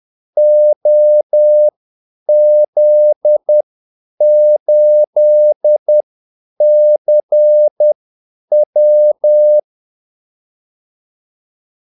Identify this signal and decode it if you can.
Morse code, heard as a keyed tone, about 10 wpm, spaced as for 8 wpm, 600 Hz. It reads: OZ8CW
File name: Callsigns-10-8wpm